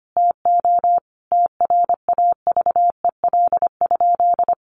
Morse code, heard as a keyed tone, about 25 wpm, 700 Hz. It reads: TO TRAVEL?
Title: Sentences from Top 1400 Words 25wpm